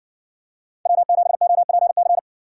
Morse code, heard as a keyed tone, about 56 wpm, 700 Hz. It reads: W7OPC